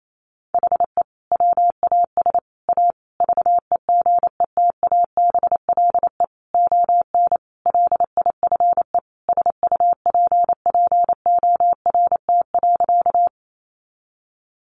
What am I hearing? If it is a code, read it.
Morse code, heard as a keyed tone, about 28 wpm, 700 Hz. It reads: HE WAS A VEGETABLE ON LIFE SUPPORT.
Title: Sentences from Top 1300 Words 28wpm